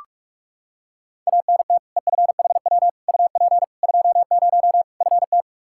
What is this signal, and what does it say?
Morse code, heard as a keyed tone, about 43 wpm, 700 Hz. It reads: ANT EFHW UP 20 FT
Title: Practice QSO - Exchange 2 - Info - 45wpm